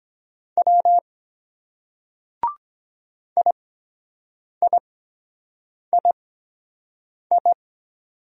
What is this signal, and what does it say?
Morse code, heard as a keyed tone, about 23 wpm, spaced as for 7 wpm, 700 Hz. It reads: W IIII